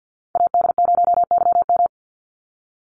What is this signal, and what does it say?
Morse code, heard as a keyed tone, about 50 wpm, 700 Hz. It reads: AD0YM
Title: NA SOTA Chaser Callsigns 50wpm